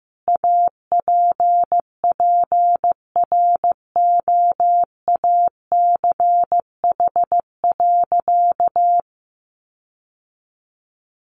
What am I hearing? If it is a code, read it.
Morse code, heard as a keyed tone, about 15 wpm, 700 Hz. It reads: APPROACH.
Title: Sentences from Top 1400 Words 15wpm